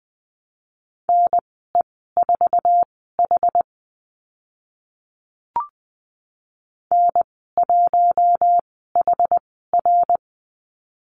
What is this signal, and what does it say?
Morse code, heard as a keyed tone, about 20 wpm, spaced as for 14 wpm, 700 Hz. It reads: NE4H N1HR
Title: 062 Lesson 11g - Letter h - Call Signs 20wpm